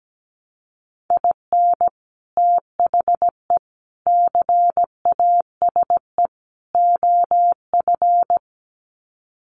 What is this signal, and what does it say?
Morse code, heard as a keyed tone, about 17 wpm, 700 Hz. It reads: IN THE CASE OF
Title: Sets of 4 Words 17wpm